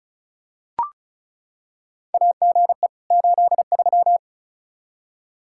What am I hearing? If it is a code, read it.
Morse code, heard as a keyed tone, about 34 wpm, 700 Hz. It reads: AGE 83
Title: Practice QSO - Exchange 3 - More Info - 35wpm